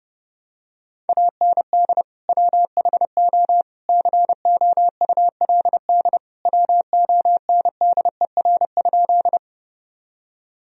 Morse code, heard as a keyed tone, about 30 wpm, 700 Hz. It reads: AND WHO COULD WONDER?
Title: Sentences from Top 1000 Words - Review Part 3 - 30wpm